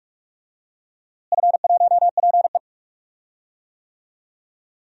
Morse code, heard as a keyed tone, about 46 wpm, 690 Hz. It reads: F1PE
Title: DXCC-100 Call Signs - 4 Characters - Repeated 3x - 45wpm